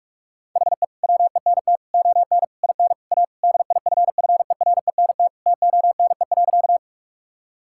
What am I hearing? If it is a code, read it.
Morse code, heard as a keyed tone, about 46 wpm, 700 Hz. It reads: HE WENT ON IN A DIFFERENT TONE.
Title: Sentences from Top 1600 Words 45wpm